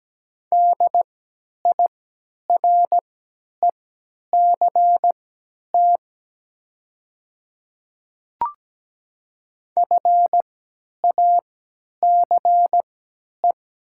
Morse code, heard as a keyed tone, about 17 wpm, spaced as for 10 wpm, 700 Hz. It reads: DIRECT FACE
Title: 3x Character Spacing - Top 500 Words - 17wpm